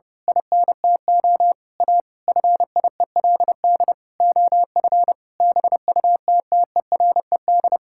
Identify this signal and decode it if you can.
Morse code, heard as a keyed tone, about 32 wpm, 700 Hz. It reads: INTO A FIELD OF BUTTERED
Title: Pangram Challenge v2 - 30wpm